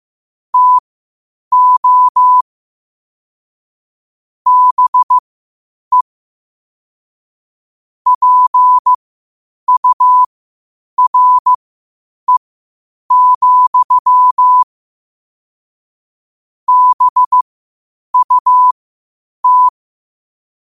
Morse code, heard as a keyed tone, about 15 wpm, spaced as for 8 wpm, 1000 Hz. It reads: TO BE PURE, BUT
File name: qotdfiles / 8wpm